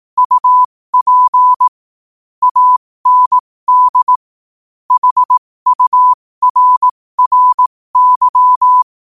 Morse code, heard as a keyed tone, about 18 wpm, spaced as for 15 wpm, 1000 Hz. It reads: UP AND HURRY
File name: qotdfiles / 15wpm